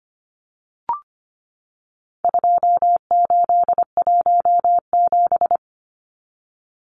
Morse code, heard as a keyed tone, about 25 wpm, 700 Hz. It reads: 2817
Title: Numbers - 4-Digit 25wpm